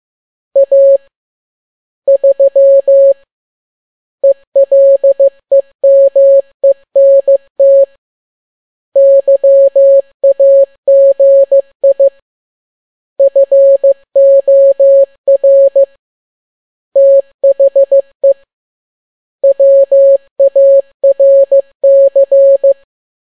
Morse code, heard as a keyed tone, about 15 wpm, 550 Hz. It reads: A 3 ELEMENT YAGI FOR THE WARC